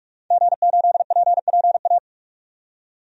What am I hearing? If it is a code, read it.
Morse code, heard as a keyed tone, about 46 wpm, 700 Hz. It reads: G8PPA
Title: DXCC-100 Call Signs - 5 Characters - 45wpm